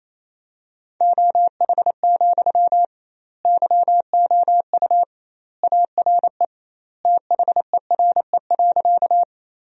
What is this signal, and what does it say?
Morse code, heard as a keyed tone, about 28 wpm, 700 Hz. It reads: OH, YOU ARE THERE.
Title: 2x Word-Spacing - Sentences from Top 500 Words - 28wpm